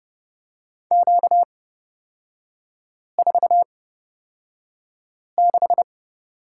Morse code, heard as a keyed tone, about 30 wpm, spaced as for 5 wpm, 700 Hz. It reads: Q46
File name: Lesson 38e - Letter q - Three Characters 30wpm